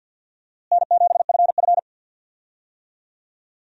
Morse code, heard as a keyed tone, about 50 wpm, 700 Hz. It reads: N7FF
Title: USA Call Signs - 4 Characters - Repeated 2x - 50wpm